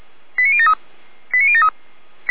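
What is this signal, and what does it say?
Five-tone selective call (ZVEI), heard as quick runs of five tones, tones of about 70 ms. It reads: 89073 89073
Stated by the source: Five-tone ZVEI selective call received on an SDR, from the feuerwehr-orlamuende.org fire brigade audio server.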